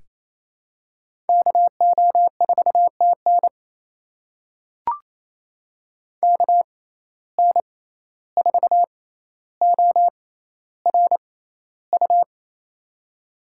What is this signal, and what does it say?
Morse code, heard as a keyed tone, about 28 wpm, spaced as for 14 wpm, 700 Hz. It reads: KO4TN KN4ORU
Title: USA Call Signs - 6x Character Spacing - 28wpm